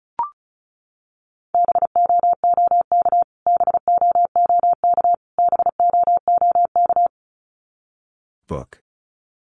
Morse code, heard as a keyed tone, about 34 wpm, 700 Hz. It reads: BOOK BOOK BOOK
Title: Top 500 Words Repeated 35wpm